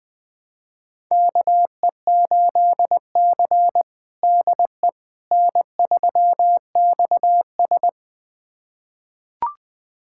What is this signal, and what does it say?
Morse code, heard as a keyed tone, about 20 wpm, 700 Hz. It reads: KE8C DE N3XS
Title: USA Callsign DE Callsign - 4 Characters - 20wpm